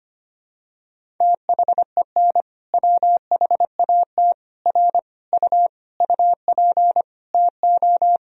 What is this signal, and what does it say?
Morse code, heard as a keyed tone, about 25 wpm, 700 Hz. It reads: THEN WHAT R U UP TO
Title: Sentences from Top 100 Words and Abbreviations 25wpm